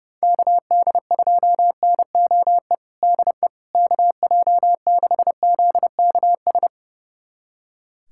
Morse code, heard as a keyed tone, about 30 wpm, 700 Hz. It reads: KD2NOE DE KJ6ZKS
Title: USA Callsign DE Callsign - 6 Characters - 30wpm